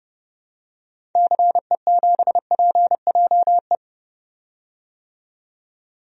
Morse code, heard as a keyed tone, about 30 wpm, 700 Hz. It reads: CE7PJE
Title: DXCC-100 Call Signs - 6 Characters - 30wpm